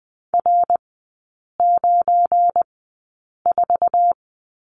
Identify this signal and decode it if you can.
Morse code, heard as a keyed tone, about 20 wpm, spaced as for 8 wpm, 700 Hz. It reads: R94
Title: 184 Lesson 30e - Letter b - Three Characters 20wpm